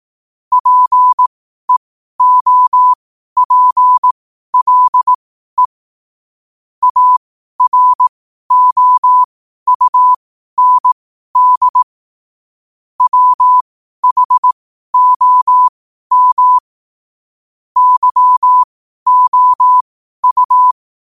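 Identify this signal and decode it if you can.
Morse code, heard as a keyed tone, about 18 wpm, spaced as for 12 wpm, 1000 Hz. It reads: PEOPLE AROUND WHOM YOU
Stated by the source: 12 wpm Morse.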